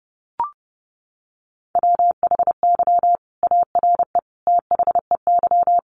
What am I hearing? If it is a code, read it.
Morse code, heard as a keyed tone, about 30 wpm, 700 Hz. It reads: WHY ARE THEY
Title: Sentences from Top 200 Words 30wpm